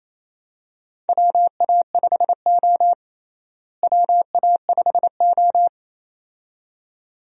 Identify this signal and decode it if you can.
Morse code, heard as a keyed tone, about 28 wpm, 700 Hz. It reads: WA5O WA5O
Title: USA Call Signs - 4 Characters - Repeated 2x - 28wpm